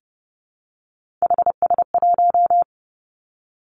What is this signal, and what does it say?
Morse code, heard as a keyed tone, about 30 wpm, 700 Hz. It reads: HS1
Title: Three Challenging Letter-Number Combinations 30wpm